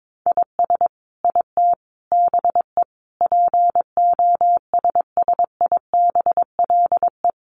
Morse code, heard as a keyed tone, about 22 wpm, 700 Hz. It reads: IS IT BE POSSIBLE
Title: Sentences from Top 1200 Words 22wpm